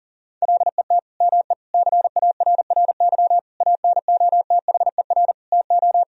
Morse code, heard as a keyed tone, about 37 wpm, 700 Hz. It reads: LET ME CARRY ANOTHER TO
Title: Sentences from Top 1000 Words - Review Part 1 - 40wpm